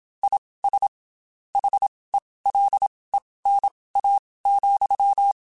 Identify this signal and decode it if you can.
Morse code, heard as a keyed tone, about 27 wpm, spaced as for 19 wpm, 800 Hz. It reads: IS HELENA,
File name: QSOs / QSOs-mp3s / WD0BGP 1@25